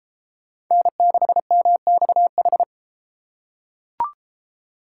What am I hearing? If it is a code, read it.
Morse code, heard as a keyed tone, about 31 wpm, 700 Hz. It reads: N6MXH